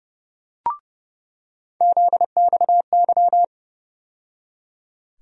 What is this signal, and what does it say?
Morse code, heard as a keyed tone, about 30 wpm, 700 Hz. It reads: ZXY
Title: USA Call Sign Postfixes - 3 Letters 30wpm